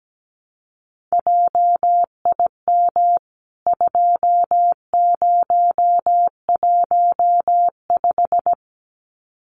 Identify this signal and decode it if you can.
Morse code, heard as a keyed tone, about 17 wpm, 700 Hz. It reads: JIM 2015
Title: CWT Exchange 17wpm